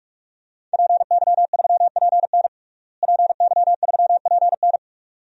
Morse code, heard as a keyed tone, about 46 wpm, 700 Hz. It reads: PY3PN PY3PN
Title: DXCC-100 Call Signs - 5 Characters - Repeated 2x - 45wpm